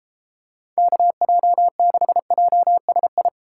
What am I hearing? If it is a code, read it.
Morse code, heard as a keyed tone, about 32 wpm, 700 Hz. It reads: KJ6JSI